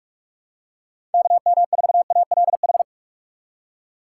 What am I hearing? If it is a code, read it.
Morse code, heard as a keyed tone, about 42 wpm, 700 Hz. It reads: KM4ALH
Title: USA Call Signs - 6 Characters - 3x Character Spacing - 45wpm